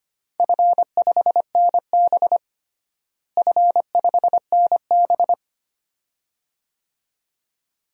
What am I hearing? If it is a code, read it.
Morse code, heard as a keyed tone, about 25 wpm, 700 Hz. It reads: F5NB F5NB
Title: DXCC-100 Call Signs - 4 Characters - Repeated 2x - 25wpm